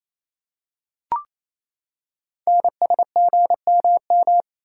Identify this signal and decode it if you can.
Morse code, heard as a keyed tone, about 27 wpm, 700 Hz. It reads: NSGMM
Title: Code Groups - 5 Letters - 28wpm